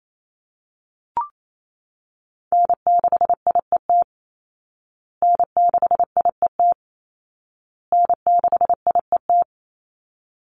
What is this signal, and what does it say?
Morse code, heard as a keyed tone, about 28 wpm, 700 Hz. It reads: N6IET N6IET N6IET
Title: Call Signs Repeated 2x - 28wpm